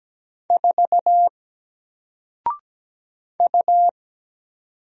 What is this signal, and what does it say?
Morse code, heard as a keyed tone, about 17 wpm, 700 Hz. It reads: 4 U
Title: Single Character (Letter:Number:Punctuation:Prosign) 17wpm